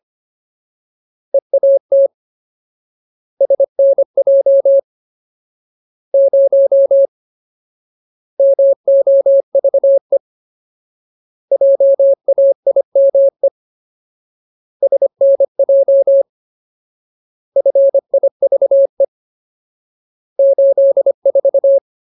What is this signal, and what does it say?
Morse code, heard as a keyed tone, about 25 wpm, 550 Hz. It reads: EAT SNJ 0 MOVE JAIME SNJ FIVE 84
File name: Ultimate Word List / 25WPM Ultimate